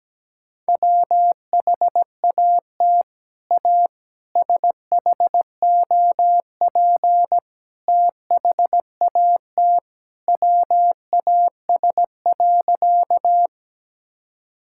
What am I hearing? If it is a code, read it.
Morse code, heard as a keyed tone, about 17 wpm, 700 Hz. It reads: WHAT A SHOP THAT WAS.
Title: Sentences from Top 1600 Words 17wpm